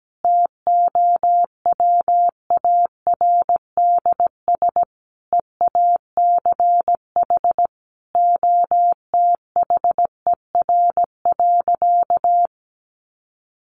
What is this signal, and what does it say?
Morse code, heard as a keyed tone, about 17 wpm, 700 Hz. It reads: TOWARDS EACH OTHER.
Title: Sentences from Top 1200 Words 17wpm